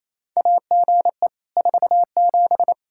Morse code, heard as a keyed tone, about 28 wpm, 700 Hz. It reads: AGE 47
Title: Practice QSO - Exchange 3 - More Info - 28wpm